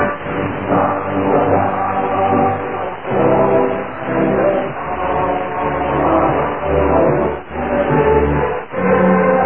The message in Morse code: none